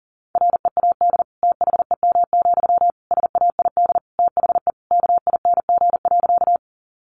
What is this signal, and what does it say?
Morse code, heard as a keyed tone, about 40 wpm, 700 Hz. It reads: READ THEM, SAID THE KING.
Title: Sentences from Top 400 Words 40wpm